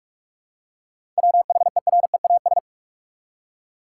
Morse code, heard as a keyed tone, about 46 wpm, 700 Hz. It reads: WHEREAS
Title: Seven Letter Words - 45wpm